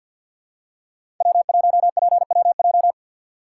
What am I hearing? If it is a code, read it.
Morse code, heard as a keyed tone, about 53 wpm, 700 Hz. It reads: W1PWJ
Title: USA Call Signs - 5 Characters - 8x Character Spacing - 50wpm